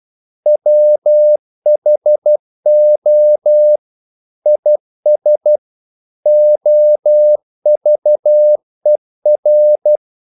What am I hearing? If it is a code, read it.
Morse code, heard as a keyed tone, about 12 wpm, 600 Hz. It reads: WHO IS OVER
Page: English quotations in CW (telegraphy - morse)